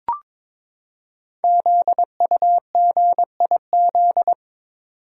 Morse code, heard as a keyed tone, about 23 wpm, 700 Hz. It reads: ZUGIZ
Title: Code Groups - 5 Letter:Numbers - 22wpm